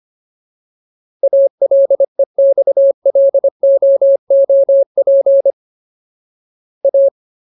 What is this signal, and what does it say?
Morse code, heard as a keyed tone, about 25 wpm, 550 Hz. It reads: ALEXLOOP A
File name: Ultimate Word List / 25WPM Ultimate